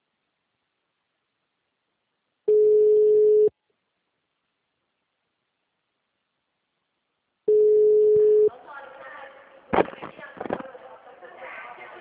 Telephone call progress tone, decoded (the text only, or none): ringback tone